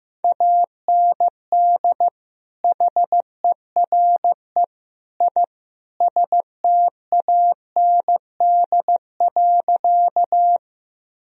Morse code, heard as a keyed tone, about 15 wpm, 700 Hz. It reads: AND HERE I STAND.
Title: Sentences from Top 1000 Words - Review Part 1 - 15wpm